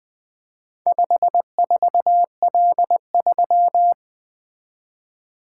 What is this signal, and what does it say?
Morse code, heard as a keyed tone, about 20 wpm, 700 Hz. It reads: 54L3
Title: Four Letter-Number Combinations - 20wpm